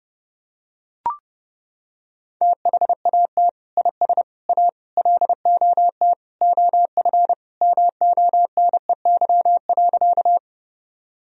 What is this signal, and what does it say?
Morse code, heard as a keyed tone, about 30 wpm, 700 Hz. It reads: THAT IS A LOT OF MONEY.
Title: Sentences from Top 1000 Words - Review Part 3 - 30wpm